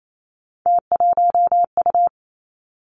Morse code, heard as a keyed tone, about 28 wpm, 700 Hz. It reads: T1U
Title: Three Challenging Letter-Number Combinations 28wpm